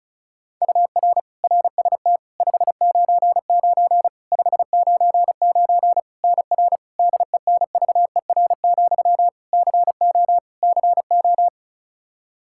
Morse code, heard as a keyed tone, about 34 wpm, 700 Hz. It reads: UR RST 599 599 NR DENVER, CO CO
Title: Practice QSO - Exchange 1 - Beginning - 35wpm